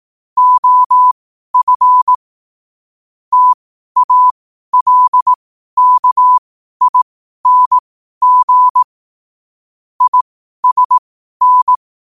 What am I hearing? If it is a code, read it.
Morse code, heard as a keyed tone, about 18 wpm, spaced as for 12 wpm, 1000 Hz. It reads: OF TALKING ISN
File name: qotdfiles / 12wpm